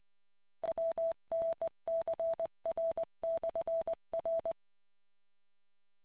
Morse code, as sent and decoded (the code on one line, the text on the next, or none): .-- -. -.-. .-. -..-. .-.
WNCR/R